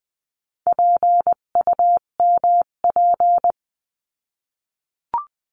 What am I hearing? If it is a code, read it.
Morse code, heard as a keyed tone, about 20 wpm, 700 Hz. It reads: PUMP
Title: 258 Lesson 42c - Words 16-20wpm